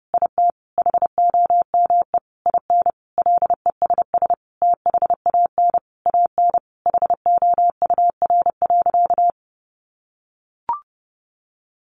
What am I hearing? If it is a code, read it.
Morse code, heard as a keyed tone, about 30 wpm, 700 Hz. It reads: IT HOME IN LESS THAN AN HOUR.
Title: Sentences from Top 500 Words 30wpm